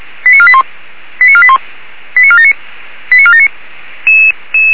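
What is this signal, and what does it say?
Five-tone selective call (ZVEI), heard as quick runs of five tones, tones of about 70 ms. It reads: 89481 89481 89479 89479
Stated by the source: ZVEI five-tone selective call received on an SDR.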